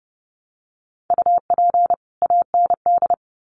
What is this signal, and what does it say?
Morse code, heard as a keyed tone, about 30 wpm, 700 Hz. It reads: UP AND
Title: Sets of 2 Words 30wpm